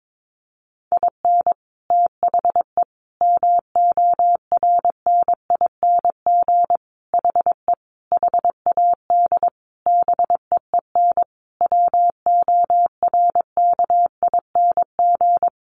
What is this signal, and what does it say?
Morse code, heard as a keyed tone, about 22 wpm, 700 Hz. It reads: IN THE MORNING HE HAD BEEN WORKING
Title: Sentences from Top 1200 Words 22wpm